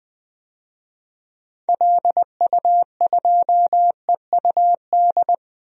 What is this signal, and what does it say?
Morse code, heard as a keyed tone, about 20 wpm, 700 Hz. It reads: LU2EUD
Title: DXCC-100 Call Signs - 6 Characters - 20wpm